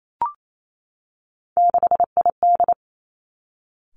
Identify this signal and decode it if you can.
Morse code, heard as a keyed tone, about 28 wpm, 700 Hz. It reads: 6ID
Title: Three Challenging Letter-Number Combinations 28wpm